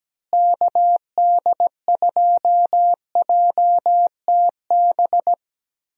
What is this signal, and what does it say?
Morse code, heard as a keyed tone, about 17 wpm, 700 Hz. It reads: KD2JTB
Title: USA Call Signs - 6 Characters - Repeated 3x - 17wpm